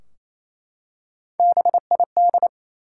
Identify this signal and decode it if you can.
Morse code, heard as a keyed tone, about 28 wpm, 700 Hz. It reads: BID
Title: Three Letter Words - 28wpm